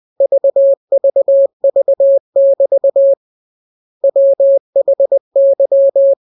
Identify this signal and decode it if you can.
Morse code, heard as a keyed tone, about 20 wpm, 550 Hz. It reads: VVV<BT> WHY
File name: jokes / joke 20 20 0000